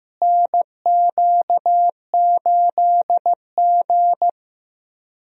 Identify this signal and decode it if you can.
Morse code, heard as a keyed tone, about 15 wpm, 700 Hz. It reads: NQ8G
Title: USA Call Signs - 4 Characters - Repeated 2x - 15wpm